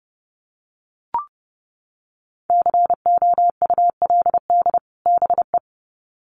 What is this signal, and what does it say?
Morse code, heard as a keyed tone, about 30 wpm, 700 Hz. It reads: COULD BE
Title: Sets of 2 Words 30wpm